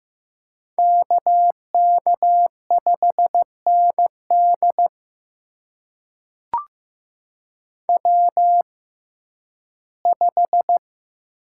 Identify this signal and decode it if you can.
Morse code, heard as a keyed tone, about 15 wpm, spaced as for 11 wpm, 700 Hz. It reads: KK5ND W5
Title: USA Call Signs - 5 Characters - 6x Character Spacing - 15wpm